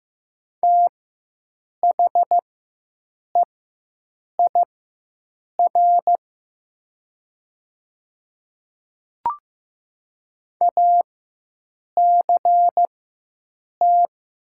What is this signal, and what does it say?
Morse code, heard as a keyed tone, about 15 wpm, spaced as for 7 wpm, 700 Hz. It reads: THEIR ACT
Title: 4x Character Spacing - Top 200 Words - 15wpm